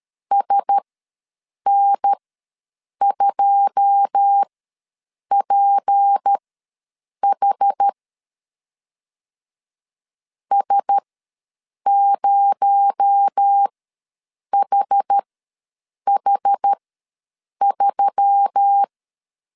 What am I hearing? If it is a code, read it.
Morse code, heard as a keyed tone, about 13 wpm, spaced as for 7 wpm, 800 Hz. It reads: SN2PH S0HH3
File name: LEZIONI / 9 28 ws 12wpm 3354